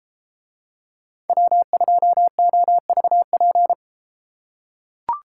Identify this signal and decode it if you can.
Morse code, heard as a keyed tone, about 33 wpm, 700 Hz. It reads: W2OVP